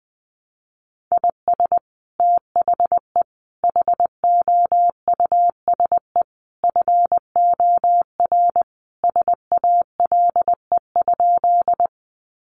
Morse code, heard as a keyed tone, about 20 wpm, 700 Hz. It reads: IS THE HOUSE FOR SALE?
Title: Sentences from Top 1400 Words 20wpm